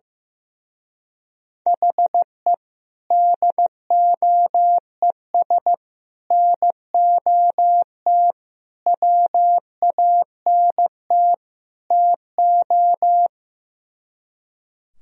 Morse code, heard as a keyed tone, about 15 wpm, 700 Hz. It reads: HE DOES NOT WANT TO
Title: Speed-Racing - Sets of 5 Words 15wpm